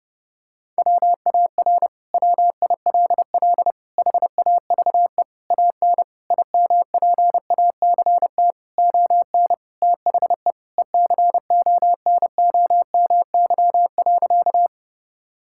Morse code, heard as a keyed tone, about 30 wpm, 700 Hz. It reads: WAR WILL HAVE AN IMPACT ON THE ECONOMY.
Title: Sentences from Top 1000 Words - Review Part 6 - 30wpm